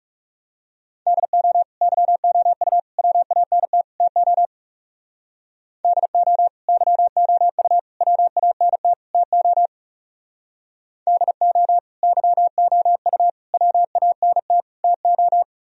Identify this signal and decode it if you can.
Morse code, heard as a keyed tone, about 37 wpm, 700 Hz. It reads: DO YOU WANT TO DO YOU WANT TO DO YOU WANT TO
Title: Speed-Racing - Sets of 4 Words 30wpm